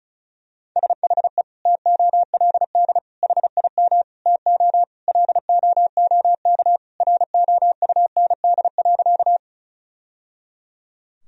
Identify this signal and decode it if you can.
Morse code, heard as a keyed tone, about 34 wpm, 700 Hz. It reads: SHE TOLD HIM TO LOOK ROUND.
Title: Sentences from Top 1000 Words - Review Part 1 - 35wpm